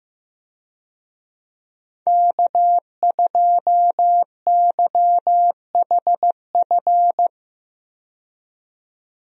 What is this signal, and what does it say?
Morse code, heard as a keyed tone, about 15 wpm, 700 Hz. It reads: K2YHF
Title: USA Call Signs - 5 Characters - 15wpm